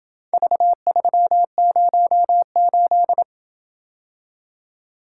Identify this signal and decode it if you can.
Morse code, heard as a keyed tone, about 28 wpm, 700 Hz. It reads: V308